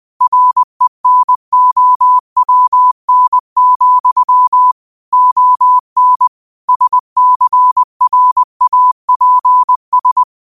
Morse code, heard as a keyed tone, about 20 wpm, 1000 Hz. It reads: RENOWN, ON SCRAPS